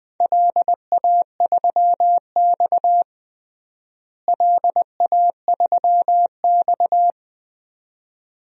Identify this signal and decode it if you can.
Morse code, heard as a keyed tone, about 21 wpm, 700 Hz. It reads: LA3X LA3X
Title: DXCC-100 Call Signs - 4 Characters - Repeated 2x - 20wpm